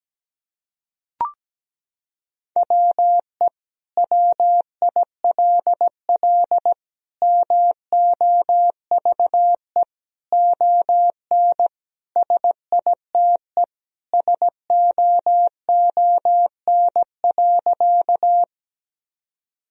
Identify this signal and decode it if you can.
Morse code, heard as a keyed tone, about 17 wpm, 700 Hz. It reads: WE WILL MOVE ON SITE SOON.
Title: Sentences from Top 1000 Words - Review Part 4 - 17wpm